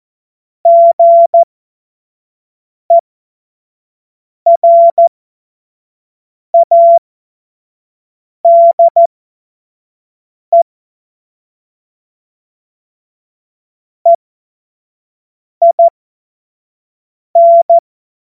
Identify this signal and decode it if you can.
Morse code, heard as a keyed tone, about 14 wpm, spaced as for 5 wpm, 680 Hz. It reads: GERADE EIN
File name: Morsekurs / Witze / Witz 12 März 2024 25 F70